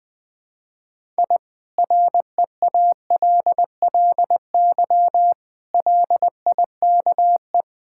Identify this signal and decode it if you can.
Morse code, heard as a keyed tone, about 19 wpm, 700 Hz. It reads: I REALLY LIKE
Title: Sentences from Top 1000 Words - Review Part 6 - 20wpm